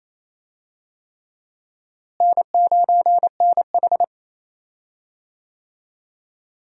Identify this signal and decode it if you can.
Morse code, heard as a keyed tone, about 28 wpm, 700 Hz. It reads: N9NH
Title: USA Call Signs - 4 Characters - Repeated 3x - 28wpm